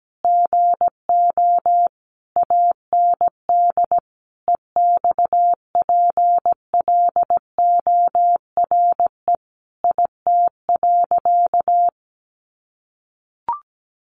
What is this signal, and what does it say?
Morse code, heard as a keyed tone, about 17 wpm, 700 Hz. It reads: GO AND EXPLORE IT.
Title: Sentences from Top 1100 Words 17wpm